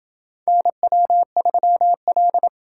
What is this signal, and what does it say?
Morse code, heard as a keyed tone, about 27 wpm, 700 Hz. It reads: NW3L